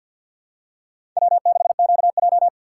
Morse code, heard as a keyed tone, about 53 wpm, 700 Hz. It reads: W6QJ
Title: USA Call Signs - 4 Characters - 6x Character Spacing - 50wpm